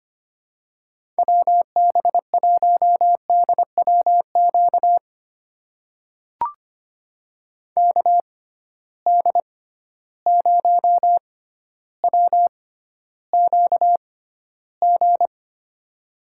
Morse code, heard as a keyed tone, about 25 wpm, spaced as for 13 wpm, 700 Hz. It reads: WB1DWQ KD0WQG
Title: USA Call Signs - 6 Characters - 6x Character Spacing - 25wpm